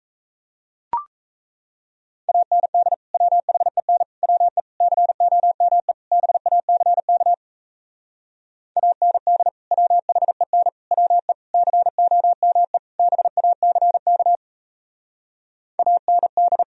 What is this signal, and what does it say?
Morse code, heard as a keyed tone, about 34 wpm, 700 Hz. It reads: AND WHEN WE COME BACK AND WHEN WE COME BACK AND
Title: Speed-Racing - Sets of 5 Words 28wpm